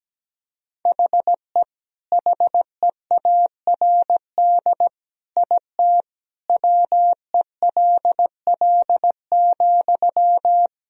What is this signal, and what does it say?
Morse code, heard as a keyed tone, about 17 wpm, 700 Hz. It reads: HE HEARD IT WELL,
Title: Sentences from Top 1000 Words - Review Part 3 - 17wpm